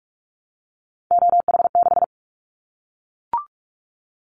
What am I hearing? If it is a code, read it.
Morse code, heard as a keyed tone, about 44 wpm, 700 Hz. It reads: OH6
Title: Callsign Prefixes for Top 25 DXCC - Repeated 3x - 45wpm